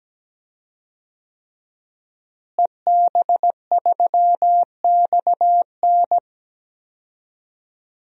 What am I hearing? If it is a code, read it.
Morse code, heard as a keyed tone, about 17 wpm, 700 Hz. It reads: EB3XN
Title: DXCC-100 Call Signs - 5 Characters - Repeated 3x - 17wpm